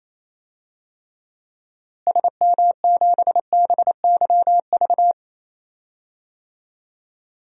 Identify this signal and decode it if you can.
Morse code, heard as a keyed tone, about 28 wpm, 700 Hz. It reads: SM7BYV
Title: DXCC-100 Call Signs - 6 Characters - Repeated 3x - 28wpm